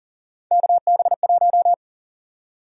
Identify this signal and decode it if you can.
Morse code, heard as a keyed tone, about 40 wpm, 700 Hz. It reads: KB1
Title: USA Call Sign Prefixes - 40wpm